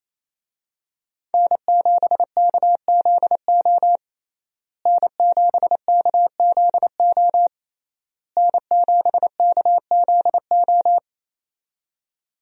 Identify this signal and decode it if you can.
Morse code, heard as a keyed tone, about 28 wpm, 700 Hz. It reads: N7KZO N7KZO N7KZO
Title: USA Call Signs - 5 Characters - Repeated 3x - 28wpm